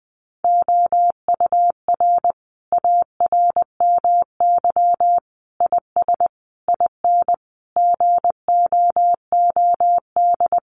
Morse code, heard as a keyed tone, about 20 wpm, 700 Hz. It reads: OUR ARMY IS IN GOOD
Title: Sentences from Top 1500 Words 20wpm